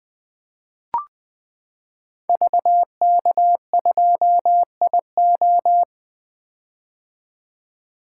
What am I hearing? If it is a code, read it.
Morse code, heard as a keyed tone, about 20 wpm, 700 Hz. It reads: VK2IO
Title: 2023 POTA Activator Call Signs - 20wpm